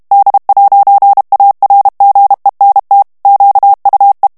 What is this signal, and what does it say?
Morse code, heard as a keyed tone, about 32 wpm, 800 Hz. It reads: D'ARGENT QUE
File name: morse mp3 / FRANCAIS TEXTE CLAIR 1 VITESSE 30 MOTS MINUTES